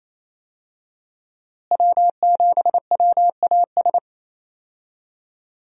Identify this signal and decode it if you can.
Morse code, heard as a keyed tone, about 28 wpm, 700 Hz. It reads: W7WAS
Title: USA Call Signs - 5 Characters - Repeated 3x - 28wpm